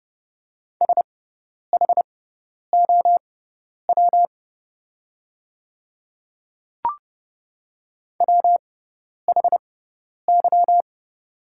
Morse code, heard as a keyed tone, about 30 wpm, spaced as for 10 wpm, 690 Hz. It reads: SHOW WHY
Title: 6x Character Spacing - Top 300 Words - 30wpm